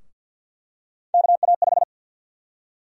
Morse code, heard as a keyed tone, about 48 wpm, 700 Hz. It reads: KA5
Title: USA Call Sign Prefixes - 50wpm